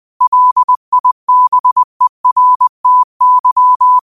Morse code, heard as a keyed tone, about 20 wpm, 1000 Hz. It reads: LIBERTY
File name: qotdfiles / 20wpm